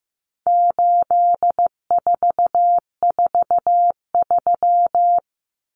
Morse code, heard as a keyed tone, about 15 wpm, 700 Hz. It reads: 8443
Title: Common Cut Numbers - 4-Digit 15wpm